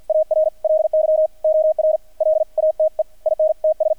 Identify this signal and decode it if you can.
Morse code, heard as a keyed tone, about 48 wpm, 620 Hz, with radio noise. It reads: ARLY OR LATE INTS
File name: archived / CRSnet